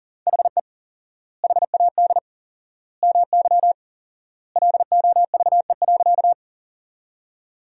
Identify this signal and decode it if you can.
Morse code, heard as a keyed tone, about 40 wpm, 700 Hz. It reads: HE HAD MY LOVE.
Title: 4x Word-Spacing - Sentences from Top 400 Words - 40wpm